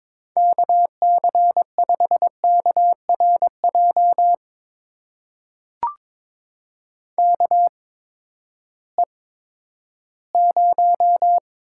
Morse code, heard as a keyed tone, about 21 wpm, spaced as for 12 wpm, 700 Hz. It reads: KC5KRJ KE0
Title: USA Call Signs - 6 Characters - 8x Character Spacing - 22wpm